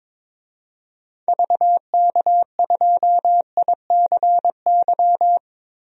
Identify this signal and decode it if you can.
Morse code, heard as a keyed tone, about 22 wpm, 700 Hz. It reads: VK2ICY
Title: DXCC-100 Call Signs - 6 Characters - 22wpm